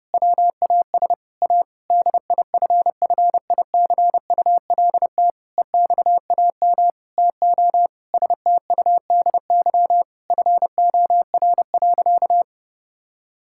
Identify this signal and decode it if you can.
Morse code, heard as a keyed tone, about 30 wpm, 700 Hz. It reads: WAS A DIFFICULT EXAM TO STUDY FOR.
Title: Sentences from Top 1000 Words - Review Part 5 - 30wpm